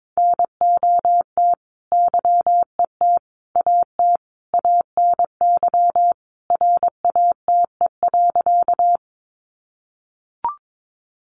Morse code, heard as a keyed tone, about 22 wpm, 700 Hz. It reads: NOT YET AT ANY RATE.
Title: Sentences from Top 700 Words 22wpm